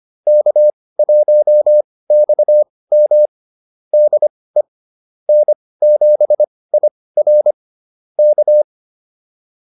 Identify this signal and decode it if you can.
Morse code, heard as a keyed tone, about 25 wpm, spaced as for 18 wpm, 600 Hz. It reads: K1XM DE N7IR K